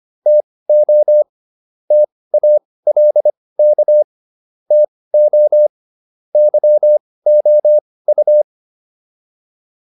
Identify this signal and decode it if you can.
Morse code, heard as a keyed tone, about 25 wpm, spaced as for 18 wpm, 600 Hz. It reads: TO TALK TO YOU